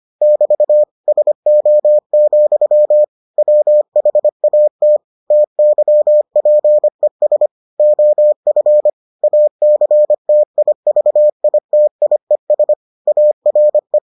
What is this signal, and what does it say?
Morse code, heard as a keyed tone, about 25 wpm, 600 Hz. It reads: <BT> SO, WHAT TYPES OF ACTIVITIES ARE